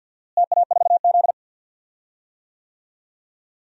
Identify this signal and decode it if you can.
Morse code, heard as a keyed tone, about 47 wpm, 700 Hz. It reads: TA4Z
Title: DXCC-100 Call Signs - 4 Characters - Repeated 2x - 50wpm